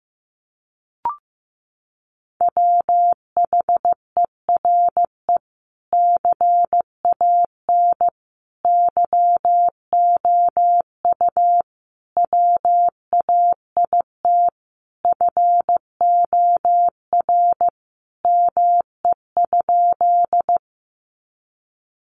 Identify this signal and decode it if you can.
Morse code, heard as a keyed tone, about 15 wpm, 700 Hz. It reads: WHERE CAN YOU WAIT FOR ME?
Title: Sentences from Top 500 Words 15wpm